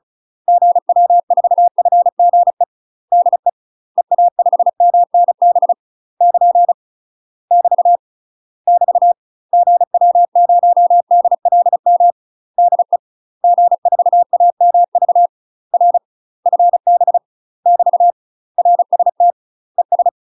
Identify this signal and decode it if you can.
Morse code, heard as a keyed tone, about 36 wpm, 700 Hz. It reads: GW4FGE DE EA5MNB <KN> <BT> <BT> GW0DLM DE G4AMV R FB <BT> RST ES
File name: UK MASTER QSOs TRAINING / 35WPM / UK MASTER QSOs 0013